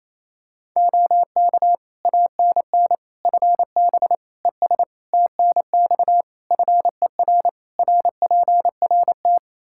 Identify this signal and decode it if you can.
Morse code, heard as a keyed tone, about 28 wpm, 700 Hz. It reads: OK ANN FB ES TNX FER RPRT
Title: K1USN SST QSO Practice 28wpm